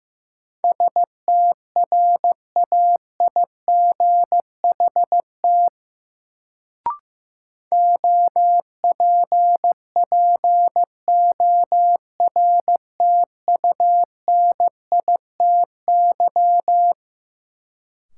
Morse code, heard as a keyed tone, about 15 wpm, 700 Hz. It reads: STRAIGHT OPPORTUNITY
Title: Sentences from Top 1000 Words - New Word Drill - 15wpm